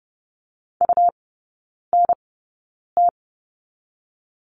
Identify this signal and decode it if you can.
Morse code, heard as a keyed tone, about 30 wpm, spaced as for 9 wpm, 690 Hz. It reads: UNT
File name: Lesson 17e - Letter u - Three Characters 30wpm